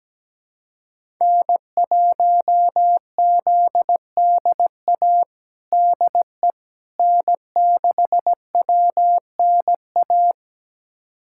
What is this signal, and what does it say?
Morse code, heard as a keyed tone, about 17 wpm, 700 Hz. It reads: N1ZDA DE N6WNA
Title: USA Callsign DE Callsign - 5 Characters - 17wpm